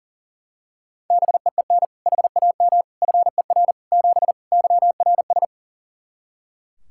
Morse code, heard as a keyed tone, about 40 wpm, 700 Hz. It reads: BEEN HAM FER 7 YRS
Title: K1USN SST QSO Practice 40wpm